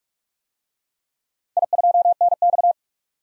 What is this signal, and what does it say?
Morse code, heard as a keyed tone, about 41 wpm, 700 Hz. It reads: I2NX
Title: DXCC-100 Call Signs - 4 Characters - 45wpm